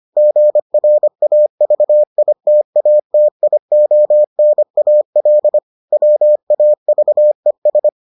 Morse code, heard as a keyed tone, about 25 wpm, 600 Hz. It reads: GRAVITATIONAL WAVES